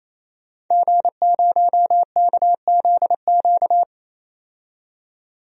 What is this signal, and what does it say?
Morse code, heard as a keyed tone, about 28 wpm, 700 Hz. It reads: G0KZQ
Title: DXCC-100 Call Signs - 5 Characters - 28wpm